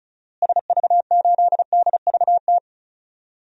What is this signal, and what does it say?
Morse code, heard as a keyed tone, about 34 wpm, 700 Hz. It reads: SV8DVT
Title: DXCC-100 Call Signs - 6 Characters - Repeated 3x - 35wpm